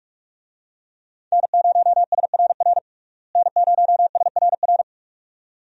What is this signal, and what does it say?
Morse code, heard as a keyed tone, about 44 wpm, 700 Hz. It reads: N0SRR N0SRR
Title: USA Call Signs - 5 Characters - Repeated 2x - 45wpm